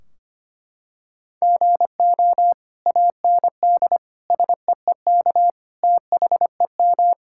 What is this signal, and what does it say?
Morse code, heard as a keyed tone, about 25 wpm, 700 Hz. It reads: GO AND SEEK THEM
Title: Sentences from Top 1600 Words 25wpm